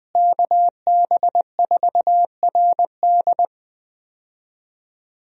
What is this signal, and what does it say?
Morse code, heard as a keyed tone, about 20 wpm, 700 Hz. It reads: KB4RD
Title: USA Call Signs - 5 Characters - Repeated 2x - 20wpm